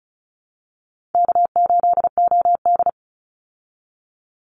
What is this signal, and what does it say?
Morse code, heard as a keyed tone, about 34 wpm, 700 Hz. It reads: K8OD
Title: Easy Call Signs Repeated 3x - 35wpm